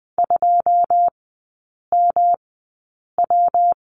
Morse code, heard as a keyed tone, about 20 wpm, spaced as for 8 wpm, 700 Hz. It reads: 2MW
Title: 114 Lesson 19e - Letter w - Three Characters 20wpm